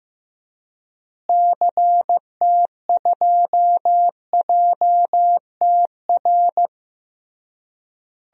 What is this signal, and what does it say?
Morse code, heard as a keyed tone, about 15 wpm, 700 Hz. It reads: CT2JTR
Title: DXCC-100 Call Signs - 6 Characters - Repeated 3x - 15wpm